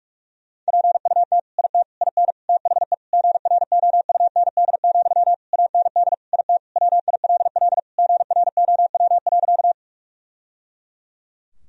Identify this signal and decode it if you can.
Morse code, heard as a keyed tone, about 46 wpm, 700 Hz. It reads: PUT IT IN THE GROUND, AND IT WILL GROW.
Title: Sentences from Top 1000 Words - Review Part 2 - 45wpm